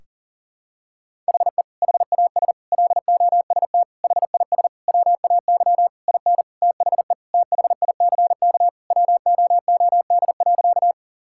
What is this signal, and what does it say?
Morse code, heard as a keyed tone, about 37 wpm, 700 Hz. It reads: HE HAS LOST HIS WAY IN THE THICK WOOD.
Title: Sentences from Top 1600 Words 40wpm